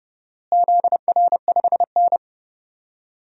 Morse code, heard as a keyed tone, about 30 wpm, 700 Hz. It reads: ZR5N
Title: DXCC-100 Call Signs - 4 Characters - Repeated 3x - 30wpm